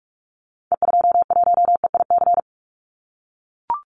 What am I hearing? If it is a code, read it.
Morse code, heard as a keyed tone, about 44 wpm, 700 Hz. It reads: E21EIC
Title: Call Signs - 45wpm